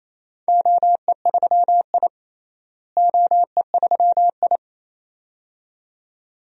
Morse code, heard as a keyed tone, about 28 wpm, 700 Hz. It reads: OE3I OE3I
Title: DXCC-100 Call Signs - 4 Characters - Repeated 2x - 28wpm